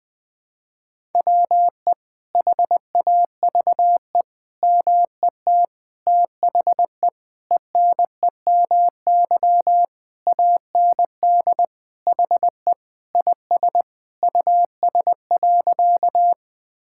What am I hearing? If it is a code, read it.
Morse code, heard as a keyed tone, about 20 wpm, 700 Hz. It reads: WE HAVE MET THE ENEMY AND HE IS US.
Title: Sentences with 9-Word Phrases - 20wpm